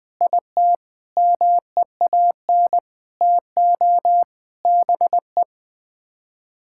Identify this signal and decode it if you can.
Morse code, heard as a keyed tone, about 20 wpm, 700 Hz. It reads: IT MEAN TO BE
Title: Sets of 6 Words - 20wpm